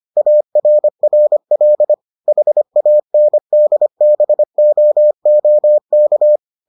Morse code, heard as a keyed tone, about 25 wpm, 600 Hz. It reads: ARRL HANDBOOK